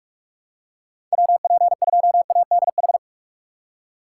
Morse code, heard as a keyed tone, about 43 wpm, 700 Hz. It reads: WP2ADH
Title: USA Call Signs - 6 Characters - 6x Character Spacing - 45wpm